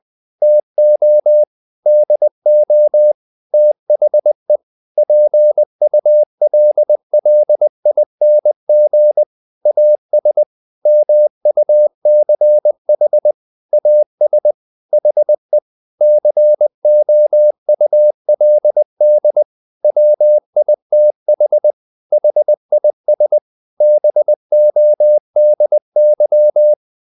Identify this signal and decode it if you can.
Morse code, heard as a keyed tone, about 20 wpm, 600 Hz. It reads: TO DO THE PULLING AS MUCH AS HE COULD WITH HIS BODY
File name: the old man and the sea part 2 20wpm